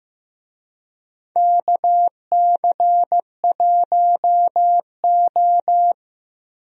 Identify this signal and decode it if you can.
Morse code, heard as a keyed tone, about 16 wpm, 700 Hz. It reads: KC1O
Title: USA Call Signs - 4 Characters - Repeated 2x - 15wpm